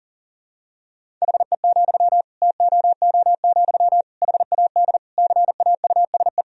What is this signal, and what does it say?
Morse code, heard as a keyed tone, about 40 wpm, 700 Hz. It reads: HE, TOO, HAD CAUSE
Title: Sentences from Top 1000 Words - Review Part 3 - 40wpm